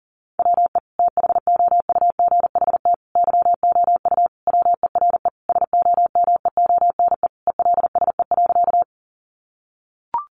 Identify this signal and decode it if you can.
Morse code, heard as a keyed tone, about 40 wpm, 700 Hz. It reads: WE THOUGHT YOU WERE SOMEONE ELSE.
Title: Sentences from Top 800 Words 40wpm